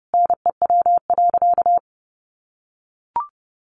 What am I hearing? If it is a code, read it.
Morse code, heard as a keyed tone, about 30 wpm, 700 Hz. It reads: NEW.
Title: Sentences from Top 1100 Words 30wpm